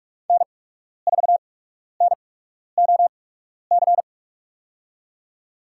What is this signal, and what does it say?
Morse code, heard as a keyed tone, about 47 wpm, spaced as for 12 wpm, 690 Hz. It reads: N4NOC
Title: USA Call Signs - 5 Characters - 8x Character Spacing - 45wpm